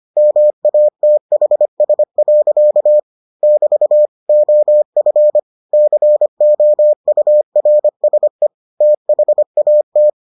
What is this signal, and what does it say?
Morse code, heard as a keyed tone, about 25 wpm, 600 Hz. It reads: MATHS. <BT> OF COURSE THAT